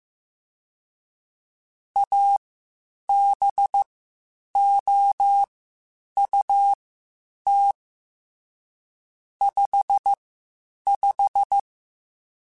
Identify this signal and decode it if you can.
Morse code, heard as a keyed tone, about 15 wpm, spaced as for 8 wpm, 800 Hz. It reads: ABOUT 55
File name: Spaced Characters / Spaced Characters-mp3s / Weather 4 24 sp @13